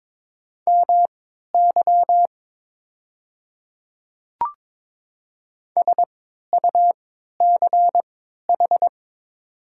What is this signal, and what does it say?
Morse code, heard as a keyed tone, about 22 wpm, spaced as for 12 wpm, 700 Hz. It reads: MY SUCH
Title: 3x Character Spacing - Top 200 Words - 22wpm